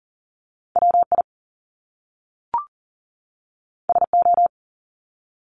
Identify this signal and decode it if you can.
Morse code, heard as a keyed tone, about 38 wpm, 700 Hz. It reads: WI SO
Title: Top Two Letter Combinations in Words 40wpm